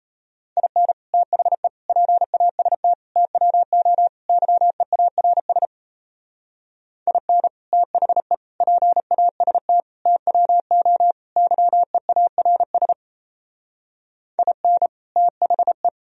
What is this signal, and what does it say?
Morse code, heard as a keyed tone, about 32 wpm, 700 Hz. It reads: IN THE PAST TWO YEARS IN THE PAST TWO YEARS IN THE
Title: Speed-Racing - Sets of 5 Words 28wpm